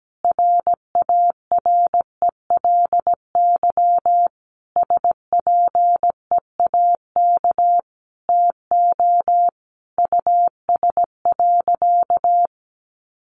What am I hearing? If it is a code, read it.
Morse code, heard as a keyed tone, about 17 wpm, 700 Hz. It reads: RARELY SPEAK TO US.
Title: Sentences from Top 1400 Words 17wpm